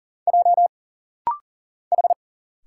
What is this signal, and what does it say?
Morse code, heard as a keyed tone, about 40 wpm, 700 Hz. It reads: J H
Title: Single Letter-Number - Rapid-Fire 40wpm